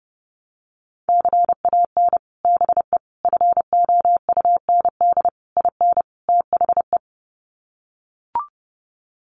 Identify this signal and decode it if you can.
Morse code, heard as a keyed tone, about 30 wpm, 700 Hz. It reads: CAN BE FOUND IN THE
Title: Sets of 5 Words Encore 30wpm